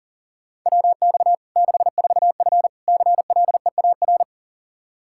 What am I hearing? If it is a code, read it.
Morse code, heard as a keyed tone, about 40 wpm, 700 Hz. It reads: WX 64F CLEAR
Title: K1USN SST QSO Practice 40wpm